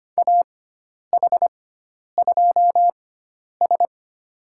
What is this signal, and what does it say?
Morse code, heard as a keyed tone, about 25 wpm, spaced as for 10 wpm, 700 Hz. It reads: AH2S
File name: Lesson 16g - Letter c - Call Signs 25wpm